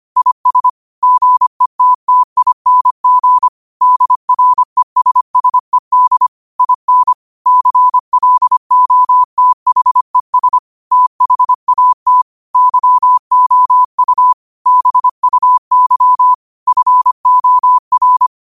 Morse code, heard as a keyed tone, about 25 wpm, 1000 Hz. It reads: IS GETTING DRESSED IN CLOTHES THAT YOU BUY FOR